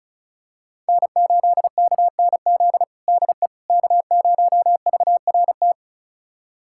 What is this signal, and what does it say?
Morse code, heard as a keyed tone, about 34 wpm, 700 Hz. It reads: N8KNZ DE K0VRT
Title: USA Callsign DE Callsign - 5 Characters - 35wpm